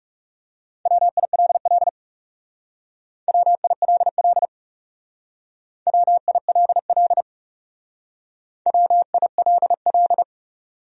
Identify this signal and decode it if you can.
Morse code, heard as a keyed tone, about 38 wpm, 700 Hz. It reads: WILL WILL WILL WILL